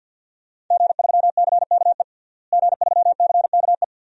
Morse code, heard as a keyed tone, about 50 wpm, 700 Hz. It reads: G3CKE G3CKE
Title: DXCC-100 Call Signs - 5 Characters - Repeated 2x - 50wpm